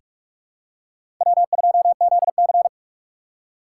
Morse code, heard as a keyed tone, about 46 wpm, 700 Hz. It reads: W2ZC
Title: USA Call Signs - 4 Characters - 6x Character Spacing - 45wpm